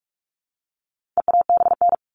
Morse code, heard as a keyed tone, about 44 wpm, 700 Hz. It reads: EABN
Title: All Cut Numbers - 4-Digit 45wpm